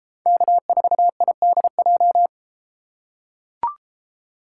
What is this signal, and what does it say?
Morse code, heard as a keyed tone, about 31 wpm, 700 Hz. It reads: K4IDJ